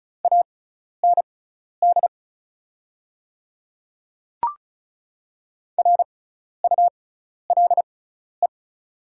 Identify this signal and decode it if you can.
Morse code, heard as a keyed tone, about 36 wpm, spaced as for 12 wpm, 690 Hz. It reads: AND RULE
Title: 6x Character Spacing - Top 500 Words - 35wpm